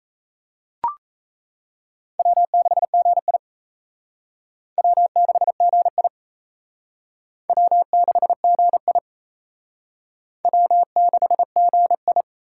Morse code, heard as a keyed tone, about 35 wpm, 700 Hz. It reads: W6GI W6GI W6GI W6GI
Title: Speed Racing - USA 1x2 Call Signs 28wpm